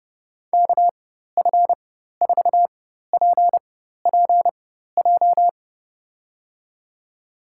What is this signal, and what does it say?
Morse code, heard as a keyed tone, about 30 wpm, spaced as for 14 wpm, 700 Hz. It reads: KF4PPJ
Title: USA Call Signs - 4x Character Spacing - 30wpm